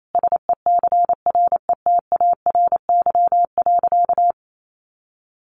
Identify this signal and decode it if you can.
Morse code, heard as a keyed tone, about 28 wpm, 700 Hz. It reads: SECRETARY.
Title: Sentences from Top 1300 Words 28wpm